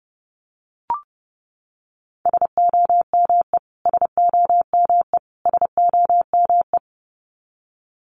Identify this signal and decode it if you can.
Morse code, heard as a keyed tone, about 30 wpm, 700 Hz. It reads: SOME SOME SOME
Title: Top 500 Words Repeated 30wpm